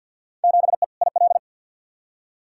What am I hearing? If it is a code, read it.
Morse code, heard as a keyed tone, about 50 wpm, 700 Hz. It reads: 7E IL
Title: Field Day Exchanges Encore - 50wpm